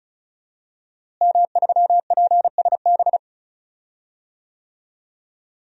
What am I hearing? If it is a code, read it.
Morse code, heard as a keyed tone, about 36 wpm, 700 Hz. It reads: M3PSB
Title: DXCC-100 Call Signs - 5 Characters - Repeated 3x - 35wpm